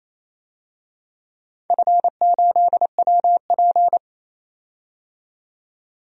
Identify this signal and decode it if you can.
Morse code, heard as a keyed tone, about 28 wpm, 700 Hz. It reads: F8WP
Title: DXCC-100 Call Signs - 4 Characters - 28wpm